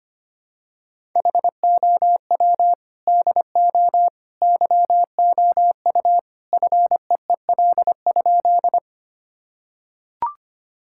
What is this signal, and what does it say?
Morse code, heard as a keyed tone, about 25 wpm, 700 Hz. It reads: HOW DO YOU FEEL?
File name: practice / 25wpm / Sentences with 3-Word Phrases - 25wpm